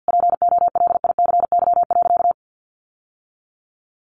Morse code, heard as a keyed tone, about 50 wpm, 700 Hz. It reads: POLICY.
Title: Sentences from Top 1200 Words 50wpm